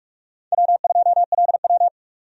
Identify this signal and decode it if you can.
Morse code, heard as a keyed tone, about 43 wpm, 700 Hz. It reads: W2LW
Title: USA Call Signs - 4 Characters - 45wpm